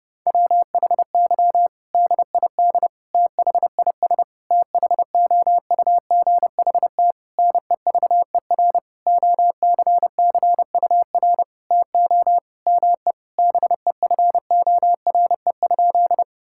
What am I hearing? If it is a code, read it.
Morse code, heard as a keyed tone, about 30 wpm, 700 Hz. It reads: WHY DID THIS THOUGHT NEVER OCCUR TO ME BEFORE?
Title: Sentences from Top 1000 Words - Review Part 6 - 30wpm